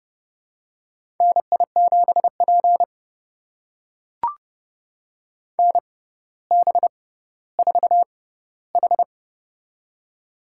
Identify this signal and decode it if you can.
Morse code, heard as a keyed tone, about 31 wpm, spaced as for 16 wpm, 700 Hz. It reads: NI7P NB4H
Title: USA Call Signs - 4 Characters - 6x Character Spacing - 30wpm